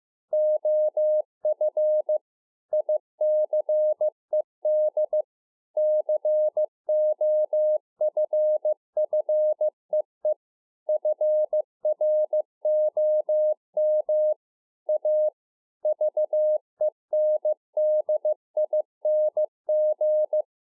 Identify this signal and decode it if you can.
Morse code, heard as a keyed tone, about 15 wpm, 610 Hz. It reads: OF ICED COFFEE FROM A VENDING